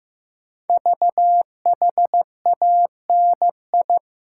Morse code, heard as a keyed tone, about 15 wpm, 700 Hz. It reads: VHANI
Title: Code Groups - 5 Letters - 15wpm